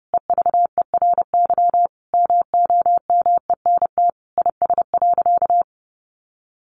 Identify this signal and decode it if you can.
Morse code, heard as a keyed tone, about 30 wpm, 700 Hz. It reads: EVERY MOMENT IS.
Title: Sentences from Top 1000 Words 30wpm